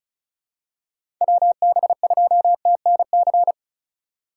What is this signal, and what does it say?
Morse code, heard as a keyed tone, about 36 wpm, 700 Hz. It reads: WB2TNC
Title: USA Call Signs - 6 Characters - 8x Character Spacing - 35wpm